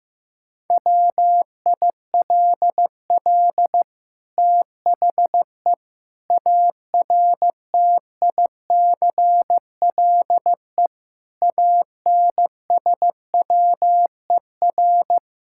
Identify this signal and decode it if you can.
Morse code, heard as a keyed tone, about 15 wpm, 700 Hz. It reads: WILL THE ARTICLE ANSWER
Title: Sentences from Top 1000 Words - Review Part 5 - 15wpm